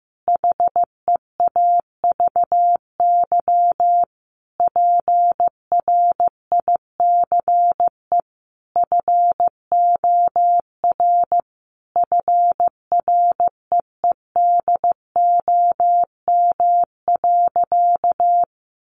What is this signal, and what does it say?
Morse code, heard as a keyed tone, about 15 wpm, 700 Hz. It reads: HEAVY PRICE FOR FREEDOM.
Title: Sentences from Top 1100 Words 15wpm